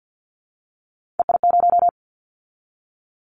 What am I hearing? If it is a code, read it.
Morse code, heard as a keyed tone, about 50 wpm, 700 Hz. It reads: EI0
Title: Callsign Prefixes for Top 50 DXCC - 50wpm